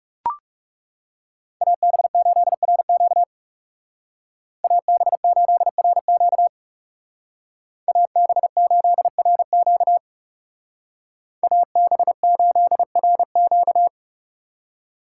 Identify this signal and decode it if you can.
Morse code, heard as a keyed tone, about 37 wpm, 700 Hz. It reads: AB8RQ AB8RQ AB8RQ AB8RQ
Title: Speed Racing - USA Call Signs 30wpm